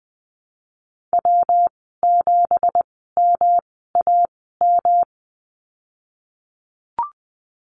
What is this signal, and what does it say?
Morse code, heard as a keyed tone, about 20 wpm, spaced as for 15 wpm, 700 Hz. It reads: W7MAM
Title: 167 Lesson 27g - Number 7 - Call Signs 20wpm